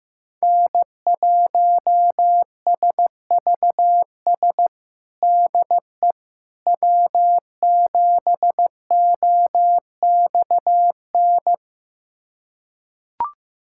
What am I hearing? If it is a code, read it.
Morse code, heard as a keyed tone, about 16 wpm, 700 Hz. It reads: N1SVS DE W7OXN
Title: USA Callsign DE Callsign - 5 Characters - 15wpm